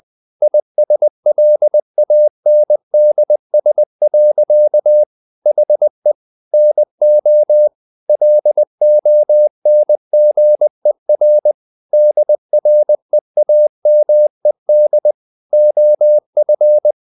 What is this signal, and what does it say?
Morse code, heard as a keyed tone, about 20 wpm, 600 Hz. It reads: ISLANDS. HE NO LONGER DREAMED OF
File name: the old man and the sea / the old man and the sea part 1 20wpm